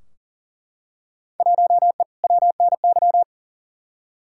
Morse code, heard as a keyed tone, about 39 wpm, 700 Hz. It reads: 1E WNY
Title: Field Day Exchanges - 40wpm